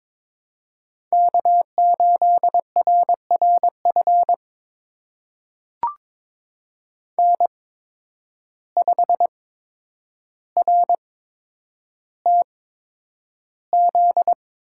Morse code, heard as a keyed tone, about 22 wpm, spaced as for 9 wpm, 700 Hz. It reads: K8RRF N5RTZ
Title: USA Call Signs - 5 Characters - 8x Character Spacing - 22wpm